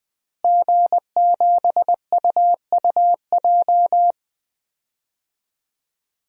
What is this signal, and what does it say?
Morse code, heard as a keyed tone, about 20 wpm, 700 Hz. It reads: G7UUJ
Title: DXCC-100 Call Signs - 5 Characters - Repeated 2x - 20wpm